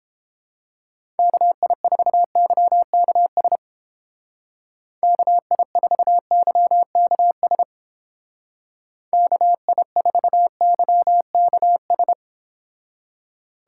Morse code, heard as a keyed tone, about 30 wpm, 700 Hz. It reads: KI4YKS KI4YKS KI4YKS